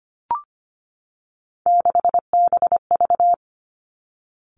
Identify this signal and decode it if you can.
Morse code, heard as a keyed tone, about 25 wpm, 700 Hz. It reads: 6BV
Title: Three Challenging Letter-Number Combinations 25wpm